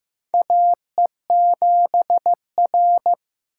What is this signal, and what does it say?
Morse code, heard as a keyed tone, about 15 wpm, 700 Hz. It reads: AE7R
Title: USA Call Signs - 3x Character Spacing - 15wpm